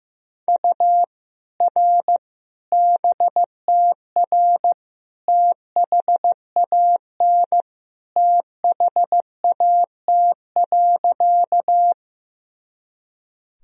Morse code, heard as a keyed tone, about 15 wpm, 700 Hz. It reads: U R BTR THAN THAT.
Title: Sentences from Top 400 Words and Abbreviations 15wpm